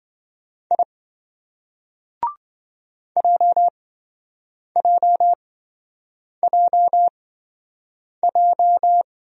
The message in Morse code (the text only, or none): I JJJJ